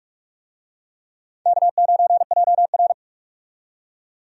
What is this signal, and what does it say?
Morse code, heard as a keyed tone, about 46 wpm, 700 Hz. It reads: K9JR
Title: USA Call Signs - 4 Characters - 45wpm